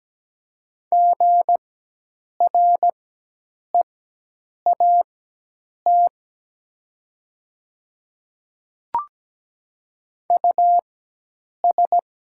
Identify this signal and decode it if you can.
Morse code, heard as a keyed tone, about 17 wpm, spaced as for 8 wpm, 700 Hz. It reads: GREAT US
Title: 4x Character Spacing - Top 300 Words - 17wpm